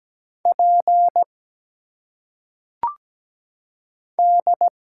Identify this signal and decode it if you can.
Morse code, heard as a keyed tone, about 17 wpm, 700 Hz. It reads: P D
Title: Single Letters - Phonetic Pronunciation 17wpm